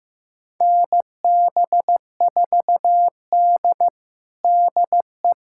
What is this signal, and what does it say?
Morse code, heard as a keyed tone, about 16 wpm, 700 Hz. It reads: NB4D DE
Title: USA Callsign DE Callsign - 4 Characters - 15wpm